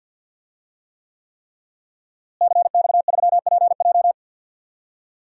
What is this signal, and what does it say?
Morse code, heard as a keyed tone, about 47 wpm, 700 Hz. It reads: CX3PJ
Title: DXCC-100 Call Signs - 5 Characters - Repeated 3x - 50wpm